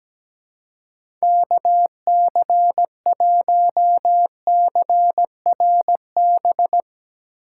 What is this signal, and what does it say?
Morse code, heard as a keyed tone, about 17 wpm, 700 Hz. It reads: KC1CRB
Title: USA Call Signs - 6 Characters - 4x Character Spacing - 17wpm